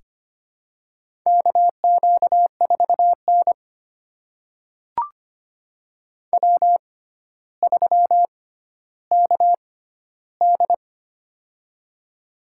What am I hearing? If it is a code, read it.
Morse code, heard as a keyed tone, about 25 wpm, spaced as for 13 wpm, 700 Hz. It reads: KQ4N W3KD
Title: USA Call Signs - 4 Characters - 6x Character Spacing - 25wpm